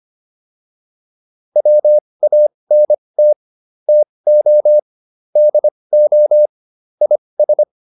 Morse code, heard as a keyed tone, about 25 wpm, spaced as for 20 wpm, 600 Hz. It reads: WANT TO DO IS